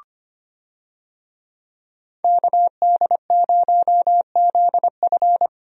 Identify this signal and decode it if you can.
Morse code, heard as a keyed tone, about 25 wpm, 700 Hz. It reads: KD0ZF
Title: USA Call Signs - 5 Characters - Repeated 3x - 25wpm